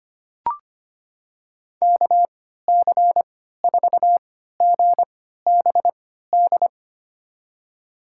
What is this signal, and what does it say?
Morse code, heard as a keyed tone, about 25 wpm, spaced as for 14 wpm, 700 Hz. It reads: KC4GBD
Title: USA Call Signs - 6 Characters - 3x Character Spacing - 25wpm